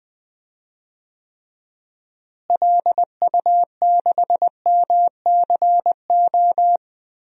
Morse code, heard as a keyed tone, about 20 wpm, 700 Hz. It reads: LU6MCO
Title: DXCC-100 Call Signs - 6 Characters - Repeated 2x - 20wpm